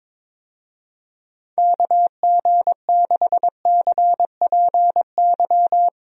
Morse code, heard as a keyed tone, about 22 wpm, 700 Hz. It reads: KG6CPY
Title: USA Call Signs - 6 Characters - 22wpm